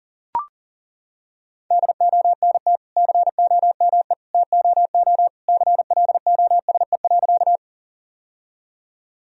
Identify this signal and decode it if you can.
Morse code, heard as a keyed tone, about 37 wpm, 700 Hz. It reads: DONT COME TOO CLOSE.
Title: Sentences from Top 1000 Words - Review Part 1 - 40wpm